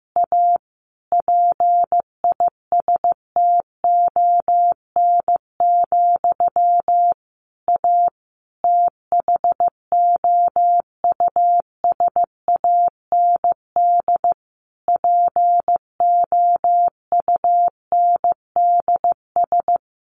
Morse code, heard as a keyed tone, about 15 wpm, 700 Hz. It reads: A PISTON, A THOUSAND POUNDS
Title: A Sound Of Thunder 15wpm